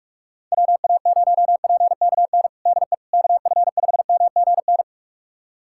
Morse code, heard as a keyed tone, about 46 wpm, 700 Hz. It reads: WA0PKN DE KF5MGN
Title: USA Callsign DE Callsign - 6 Characters - 45wpm